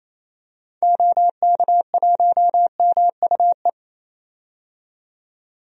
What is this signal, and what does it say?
Morse code, heard as a keyed tone, about 28 wpm, 700 Hz. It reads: OK1MUE
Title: DXCC-100 Call Signs - 6 Characters - 28wpm